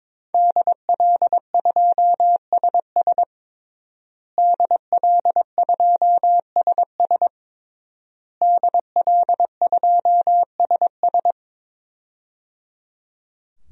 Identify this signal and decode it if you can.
Morse code, heard as a keyed tone, about 22 wpm, 700 Hz. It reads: DL2SS DL2SS DL2SS
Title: DXCC-100 Call Signs - 5 Characters - Repeated 3x - 22wpm